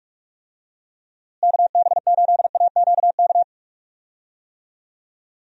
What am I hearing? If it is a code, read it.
Morse code, heard as a keyed tone, about 48 wpm, 700 Hz. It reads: KB8AQK
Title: USA Call Signs - 6 Characters - Repeated 3x - 45wpm